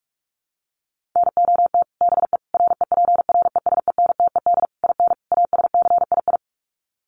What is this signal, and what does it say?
Morse code, heard as a keyed tone, about 44 wpm, 700 Hz. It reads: NOT BE REPRESENTED IN ASCII
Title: Sets of 5 Words 45wpm